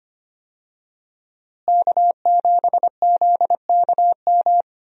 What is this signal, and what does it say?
Morse code, heard as a keyed tone, about 25 wpm, 700 Hz. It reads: K7ZKM
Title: USA Call Signs - 5 Characters - Repeated 2x - 25wpm